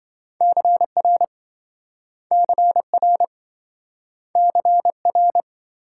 Morse code, heard as a keyed tone, about 27 wpm, 700 Hz. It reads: CR CR CR